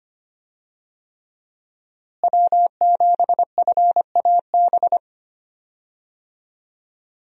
Morse code, heard as a keyed tone, about 25 wpm, 700 Hz. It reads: W7FAB
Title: USA Call Signs - 5 Characters - Repeated 3x - 25wpm